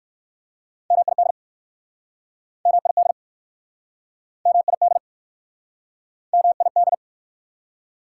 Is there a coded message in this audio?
Yes: Morse code from 0.9 to 6.9 s, heard as a keyed tone, about 61 wpm, 700 Hz. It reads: MID MID MID MID